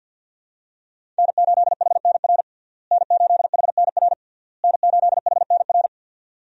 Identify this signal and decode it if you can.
Morse code, heard as a keyed tone, about 50 wpm, 700 Hz. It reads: N8HNR N8HNR N8HNR
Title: USA Call Signs - 5 Characters - Repeated 3x - 50wpm